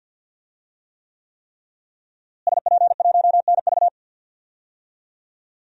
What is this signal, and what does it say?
Morse code, heard as a keyed tone, about 47 wpm, 700 Hz. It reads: SP1NV
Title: DXCC-100 Call Signs - 5 Characters - Repeated 3x - 50wpm